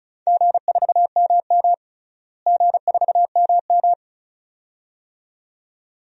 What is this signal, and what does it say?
Morse code, heard as a keyed tone, about 36 wpm, 700 Hz. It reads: G4MM G4MM
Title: DXCC-100 Call Signs - 4 Characters - Repeated 2x - 35wpm